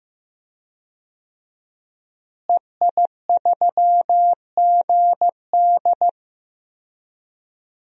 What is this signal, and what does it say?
Morse code, heard as a keyed tone, about 16 wpm, 700 Hz. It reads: EI3GD
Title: DXCC-100 Call Signs - 5 Characters - Repeated 2x - 15wpm